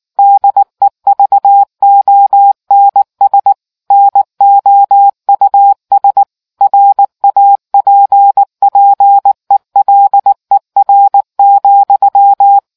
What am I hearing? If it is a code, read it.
Morse code, heard as a keyed tone, about 19 wpm, 800 Hz. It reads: DEVONS NOUS RAPPELER,